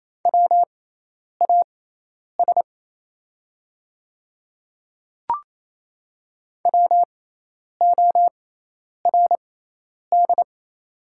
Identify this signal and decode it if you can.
Morse code, heard as a keyed tone, about 28 wpm, spaced as for 10 wpm, 700 Hz. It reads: WAS WORD
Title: 6x Character Spacing - Top 100 Words - 28wpm